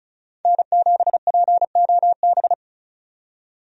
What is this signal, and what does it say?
Morse code, heard as a keyed tone, about 36 wpm, 700 Hz. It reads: N7POB
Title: USA Call Signs - 5 Characters - 35wpm